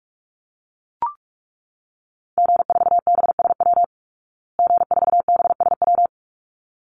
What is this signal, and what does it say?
Morse code, heard as a keyed tone, about 44 wpm, 700 Hz. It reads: G4BSW G4BSW
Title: Call Signs Repeated 3x - 45wpm